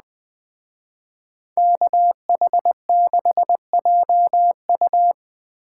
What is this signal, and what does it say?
Morse code, heard as a keyed tone, about 20 wpm, 700 Hz. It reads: KH6JU
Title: Speed Racing - USA 2x2 Call Signs 17wpm